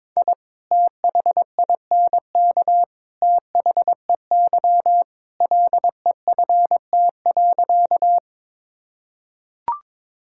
Morse code, heard as a keyed tone, about 21 wpm, 700 Hz. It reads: I THINK THEY LEFT.
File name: practice / 22wpm / Sentences with 3-Word Phrases - 22wpm